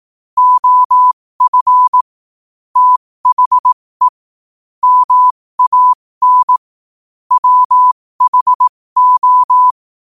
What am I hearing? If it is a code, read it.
Morse code, heard as a keyed tone, about 18 wpm, spaced as for 15 wpm, 1000 Hz. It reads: OF THE MAN WHO
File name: qotdfiles / 15wpm